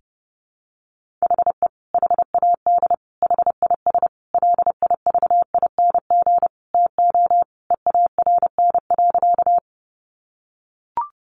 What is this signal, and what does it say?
Morse code, heard as a keyed tone, about 28 wpm, 700 Hz. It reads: HE HAD HIS LIVING TO EARN.
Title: Sentences from Top 1500 Words 30wpm